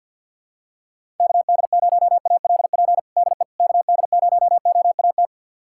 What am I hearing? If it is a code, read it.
Morse code, heard as a keyed tone, about 46 wpm, 700 Hz. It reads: KD0ALP DE KD0OAT
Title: USA Callsign DE Callsign 50wpm